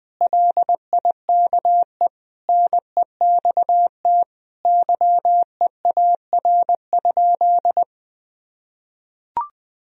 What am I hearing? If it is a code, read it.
Morse code, heard as a keyed tone, about 20 wpm, 700 Hz. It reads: LIKE NEXT YEAR?
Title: Sentences from Top 1000 Words - Review Part 3 - 20wpm